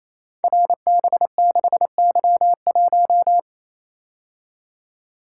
Morse code, heard as a keyed tone, about 28 wpm, 700 Hz. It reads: RB6Y1
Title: Code Groups - 5 Letter:Numbers - 28wpm